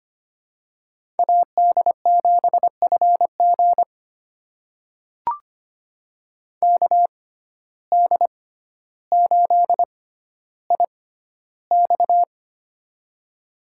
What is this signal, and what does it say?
Morse code, heard as a keyed tone, about 25 wpm, spaced as for 13 wpm, 700 Hz. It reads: AD7FG KD8IX
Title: USA Call Signs - 5 Characters - 6x Character Spacing - 25wpm